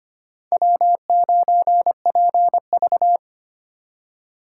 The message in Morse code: W9PV

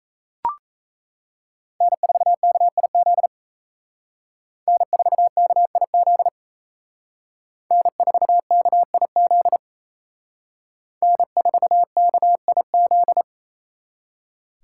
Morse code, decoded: N4KIZ N4KIZ N4KIZ N4KIZ

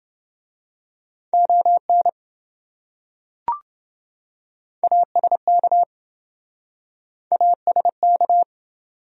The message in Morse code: ON ASK ASK

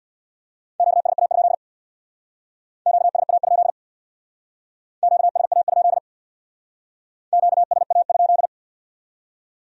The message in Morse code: QSA? QSA? QSA? QSA?

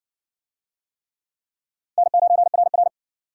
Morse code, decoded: N9RR

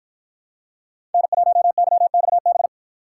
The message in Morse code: N1YXB

